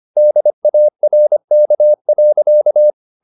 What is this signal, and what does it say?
Morse code, heard as a keyed tone, about 25 wpm, 600 Hz. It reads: DARK.
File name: the old man and the sea part 1 25wpm